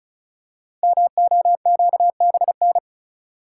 Morse code, heard as a keyed tone, about 35 wpm, 700 Hz. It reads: MOQBN
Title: Code Groups - 5 Letters - 35wpm